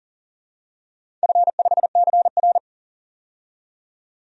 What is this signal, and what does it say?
Morse code, heard as a keyed tone, about 40 wpm, 700 Hz. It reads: F5CR
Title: DXCC-100 Call Signs - 4 Characters - 40wpm